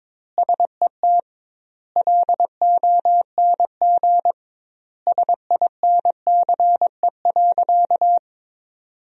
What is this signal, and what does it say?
Morse code, heard as a keyed tone, about 22 wpm, 700 Hz. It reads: SET LONG SINCE.
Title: 2x Word-Spacing - Sentences from Top 300 Words - 22wpm